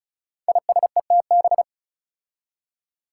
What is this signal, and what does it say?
Morse code, heard as a keyed tone, about 36 wpm, 700 Hz. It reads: ISETB
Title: Five Challenging Letter-Number Combinations 35wpm